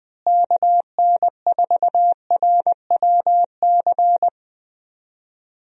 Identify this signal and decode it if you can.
Morse code, heard as a keyed tone, about 21 wpm, 700 Hz. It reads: KN4RWC
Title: USA Call Signs - 6 Characters - Repeated 2x - 20wpm